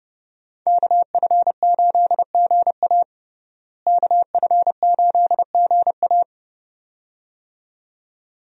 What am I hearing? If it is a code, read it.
Morse code, heard as a keyed tone, about 32 wpm, 700 Hz. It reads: KF8GA KF8GA
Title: USA Call Signs - 5 Characters - Repeated 2x - 30wpm